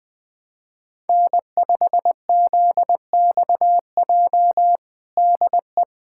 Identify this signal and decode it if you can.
Morse code, heard as a keyed tone, about 21 wpm, 700 Hz. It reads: N5ZXJ DE
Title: USA Callsign DE Callsign - 5 Characters - 20wpm